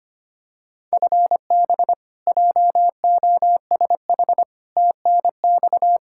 Morse code, heard as a keyed tone, about 25 wpm, 700 Hz. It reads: FB JOSH TNX
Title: Practice QSO - Exchange 3 - More Info - 25wpm